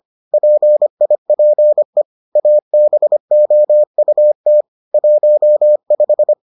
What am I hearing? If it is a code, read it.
Morse code, heard as a keyed tone, about 25 wpm, 600 Hz. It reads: PIPE ABOUT 15